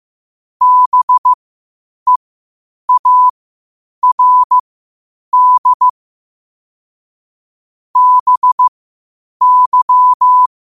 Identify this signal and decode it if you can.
Morse code, heard as a keyed tone, about 15 wpm, spaced as for 8 wpm, 1000 Hz. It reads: BEARD BY
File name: qotdfiles / 8wpm